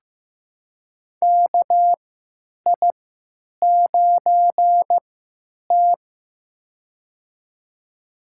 Morse code, heard as a keyed tone, about 15 wpm, spaced as for 9 wpm, 700 Hz. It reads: KI9T
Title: USA Call Signs - 4 Characters - 3x Character Spacing - 15wpm